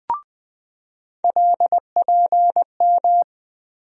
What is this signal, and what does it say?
Morse code, heard as a keyed tone, about 20 wpm, 700 Hz. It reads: LPM
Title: USA Call Sign Postfixes - 3 Letters 20wpm